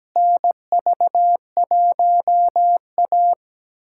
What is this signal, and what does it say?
Morse code, heard as a keyed tone, about 17 wpm, 700 Hz. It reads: NV1A
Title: USA Callsign DE Callsign - 4 Characters - 17wpm